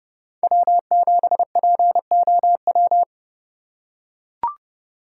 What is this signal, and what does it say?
Morse code, heard as a keyed tone, about 31 wpm, 700 Hz. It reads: W7POW
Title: USA Call Signs - 5 Characters - 3x Character Spacing - 30wpm